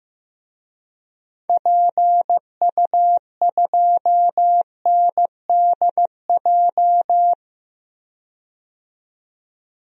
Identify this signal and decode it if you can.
Morse code, heard as a keyed tone, about 15 wpm, 700 Hz. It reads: PU2NDJ
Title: DXCC-100 Call Signs - 6 Characters - 15wpm